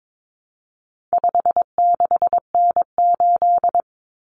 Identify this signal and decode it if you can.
Morse code, heard as a keyed tone, about 22 wpm, 700 Hz. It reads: 56N8
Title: Common Cut Numbers - 4-Digit 22wpm